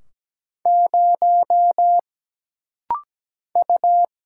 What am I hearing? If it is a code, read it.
Morse code, heard as a keyed tone, about 17 wpm, 700 Hz. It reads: 0 U
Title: Single Letter-Number - Rapid-Fire 17wpm